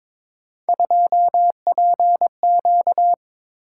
Morse code, heard as a keyed tone, about 23 wpm, 700 Hz. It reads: 2PQ
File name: practice / 22wpm / Three Letter-Number Combinations - 22wpm